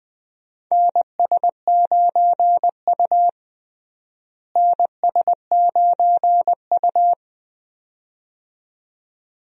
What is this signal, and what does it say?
Morse code, heard as a keyed tone, about 20 wpm, 700 Hz. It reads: NS9U NS9U
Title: USA Call Signs - 4 Characters - Repeated 2x - 20wpm